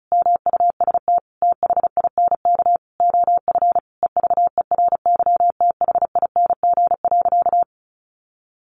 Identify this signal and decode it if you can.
Morse code, heard as a keyed tone, about 34 wpm, 700 Hz. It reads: MUST THINK OF EVERYTHING.
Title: Sentences from Top 1000 Words 35wpm